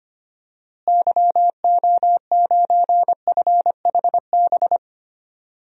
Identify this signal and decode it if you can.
Morse code, heard as a keyed tone, about 25 wpm, 700 Hz. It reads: YO9FHB
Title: DXCC-100 Call Signs - 6 Characters - Repeated 2x - 25wpm